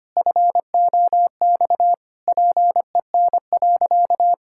Morse code, heard as a keyed tone, about 25 wpm, 700 Hz. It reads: FOX PEN.
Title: Pangram Challenge v2 - 25wpm